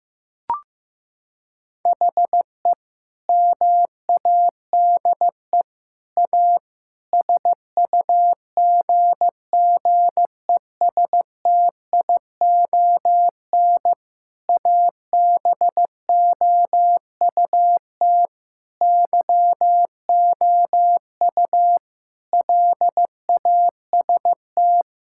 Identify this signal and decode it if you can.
Morse code, heard as a keyed tone, about 15 wpm, 700 Hz. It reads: HE MADE A SUGGESTION ABOUT YOU LAST
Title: Sentences from Top 1600 Words 15wpm